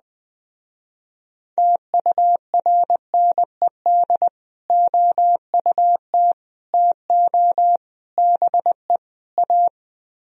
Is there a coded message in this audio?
Yes: Morse code from 1.6 to 9.7 s, heard as a keyed tone, about 20 wpm, 700 Hz. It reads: TURNED OUT TO BE A